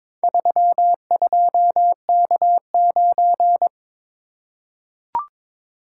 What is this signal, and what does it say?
Morse code, heard as a keyed tone, about 22 wpm, 700 Hz. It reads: 32K9
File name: practice / 22wpm / Four Letter-Number Combinations - 22wpm